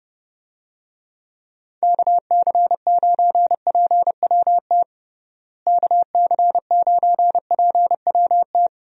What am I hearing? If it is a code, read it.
Morse code, heard as a keyed tone, about 30 wpm, 700 Hz. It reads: KC9PWT KC9PWT
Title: USA Call Signs - 6 Characters - Repeated 2x - 30wpm